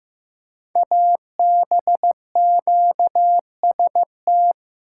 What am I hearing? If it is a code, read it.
Morse code, heard as a keyed tone, about 15 wpm, 700 Hz. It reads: ABQST
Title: Code Groups - 5 Letters - 15wpm